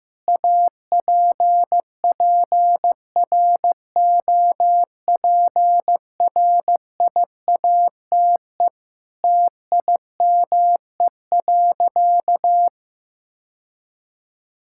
Morse code, heard as a keyed tone, about 15 wpm, 700 Hz. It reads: APPROPRIATE TIME.
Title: Sentences from Top 1000 Words - Review Part 5 - 15wpm